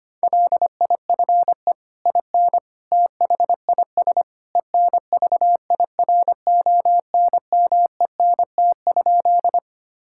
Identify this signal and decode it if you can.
Morse code, heard as a keyed tone, about 25 wpm, 700 Hz. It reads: LIFE IN THIS ENVIRONMENT?
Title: Sentences from Top 1000 Words - Review Part 5 - 25wpm